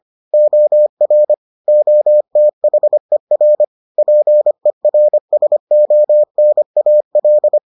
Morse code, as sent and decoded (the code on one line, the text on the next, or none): --- .-. / --- - .... .-. / .--. .-. ... --- -. .- .-..
OR OTHER PERSONAL